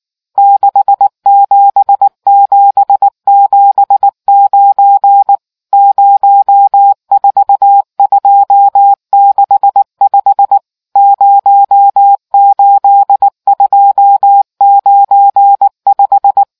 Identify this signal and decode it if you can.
Morse code, heard as a keyed tone, about 19 wpm, 800 Hz. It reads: 67779 04265 08295